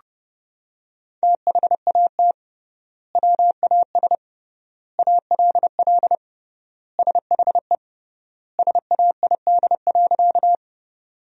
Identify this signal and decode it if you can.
Morse code, heard as a keyed tone, about 32 wpm, 700 Hz. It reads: THAT WAS ALL SHE SAID.
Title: 3x Word-Spacing - Sentences from Top 100 Words - 30wpm